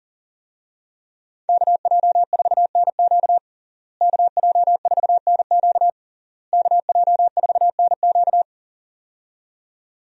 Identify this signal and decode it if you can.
Morse code, heard as a keyed tone, about 40 wpm, 700 Hz. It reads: KJ4NQ KJ4NQ KJ4NQ
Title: USA Call Signs - 5 Characters - Repeated 3x - 40wpm